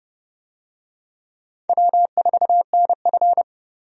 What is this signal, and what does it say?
Morse code, heard as a keyed tone, about 29 wpm, 700 Hz. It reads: W4NF
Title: USA Call Signs - 4 Characters - Repeated 2x - 30wpm